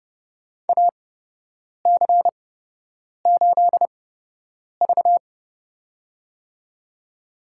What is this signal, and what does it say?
Morse code, heard as a keyed tone, about 32 wpm, spaced as for 8 wpm, 700 Hz. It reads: AC8V
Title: USA Call Signs - 8x Character Spacing - 30wpm